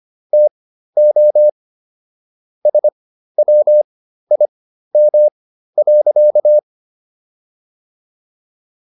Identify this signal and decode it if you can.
Morse code, heard as a keyed tone, about 25 wpm, spaced as for 13 wpm, 600 Hz. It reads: TO SWIM.